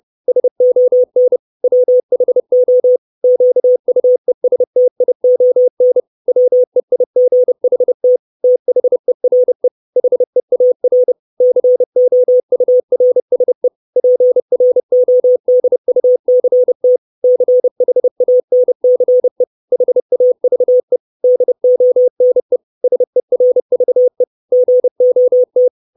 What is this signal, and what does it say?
Morse code, heard as a keyed tone, about 30 wpm, 500 Hz. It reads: SON WHO QUESTION WEIGHT THERE HEAR COURSE PRODUCT CHANCE HAVE DONE SERVE GOT
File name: 632 Words 30WPM